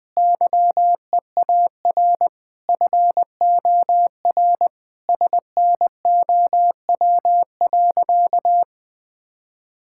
Morse code, heard as a keyed tone, about 20 wpm, 700 Hz. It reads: YEAR FOR SNOW.
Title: Sentences with 4-Word Phrases - 20wpm